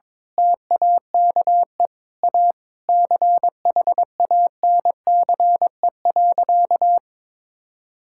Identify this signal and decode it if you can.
Morse code, heard as a keyed tone, about 21 wpm, 700 Hz. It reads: TAKE A CHANCE.
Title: Sentences from Top 1000 Words - Review Part 6 - 22wpm